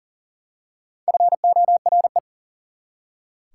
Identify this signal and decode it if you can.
Morse code, heard as a keyed tone, about 40 wpm, 700 Hz. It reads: FORE
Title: Prefixes and Suffixes 40wpm